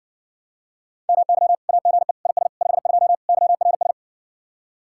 Morse code, heard as a keyed tone, about 60 wpm, 700 Hz. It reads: MY AGE IS 52 YRS